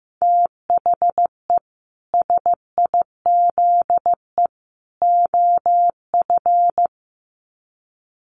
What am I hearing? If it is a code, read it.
Morse code, heard as a keyed tone, about 15 wpm, 700 Hz. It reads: THE SIZE OF
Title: Sets of 3 Words 15wpm